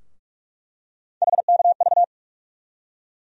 Morse code, heard as a keyed tone, about 41 wpm, 700 Hz. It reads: HKV
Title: USA Call Sign Postfixes - 3 Letters 45wpm